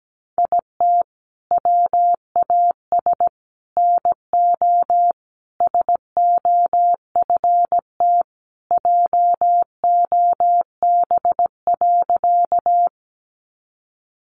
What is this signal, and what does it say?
Morse code, heard as a keyed tone, about 17 wpm, 700 Hz. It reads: IT WAS NO SOFT JOB.
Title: Sentences from Top 1500 Words 17wpm